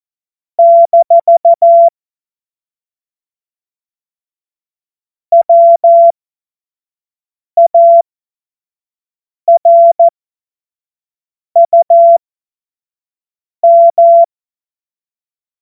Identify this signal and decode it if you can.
Morse code, heard as a keyed tone, about 14 wpm, spaced as for 5 wpm, 680 Hz. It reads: - WARUM
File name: Morsekurs / Witze / Witz 06. Juli 2021, 25 F70